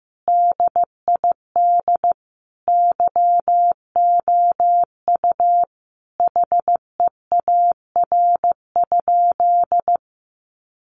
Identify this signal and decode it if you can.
Morse code, heard as a keyed tone, about 15 wpm, 700 Hz. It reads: DID YOU HEAR?
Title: Sentences from Top 300 Words 15wpm